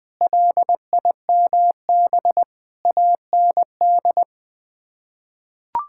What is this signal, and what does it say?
Morse code, heard as a keyed tone, about 20 wpm, 700 Hz. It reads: LIMB AND
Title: Sets of 8 Words - 20wpm